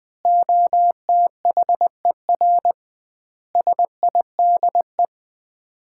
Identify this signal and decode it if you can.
Morse code, heard as a keyed tone, about 21 wpm, 700 Hz. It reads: OTHER SIDE
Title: 2x Word-Spacing - Sets of 4 Words - 20wpm